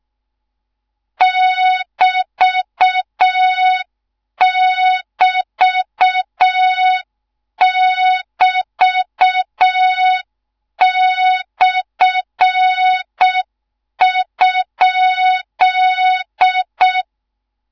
Morse code, heard as a keyed tone, about 6 wpm, 3700 Hz, with noise. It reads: <BT><BT><BT>/?
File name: Morze-HA2MN / hang / 13IRASJELEK1 HA2MN